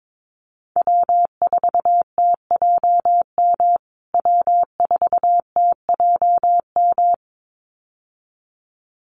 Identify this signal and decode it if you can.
Morse code, heard as a keyed tone, about 22 wpm, 700 Hz. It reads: W4TJM W4TJM
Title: USA Call Signs Repeated 2x 22wpm